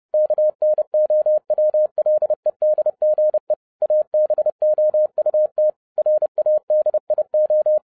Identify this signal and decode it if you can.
Morse code, heard as a keyed tone, about 30 wpm, 610 Hz. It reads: KNOWLEDGE ABOUT RADIO